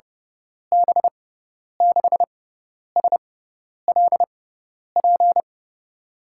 Morse code, heard as a keyed tone, about 32 wpm, spaced as for 10 wpm, 700 Hz. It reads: B6SLP
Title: USA Call Signs - 6 Characters - 6x Character Spacing - 30wpm